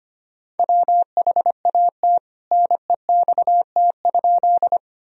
Morse code, heard as a keyed tone, about 25 wpm, 700 Hz. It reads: WHAT NEXT?
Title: Sentences from Top 1000 Words - Review Part 1 - 25wpm